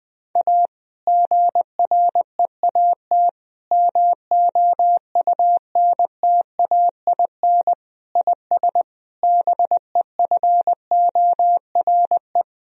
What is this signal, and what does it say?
Morse code, heard as a keyed tone, about 21 wpm, 700 Hz. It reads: A GREAT MOUNTAIN IS BEFORE
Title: Sentences from Top 1000 Words - Review Part 1 - 20wpm